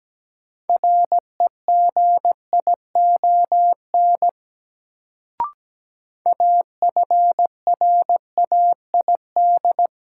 Morse code, heard as a keyed tone, about 17 wpm, 700 Hz. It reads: REGION AFRAID
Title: Sentences from Top 1500 Words - New Word Drill - 17wpm